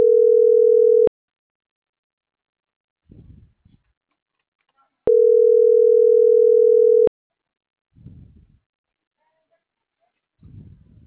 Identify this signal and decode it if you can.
Ringback tone (North America), 440 + 480 Hz.
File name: CENTRAL MAERSK / out gabrielle.melendez - .28643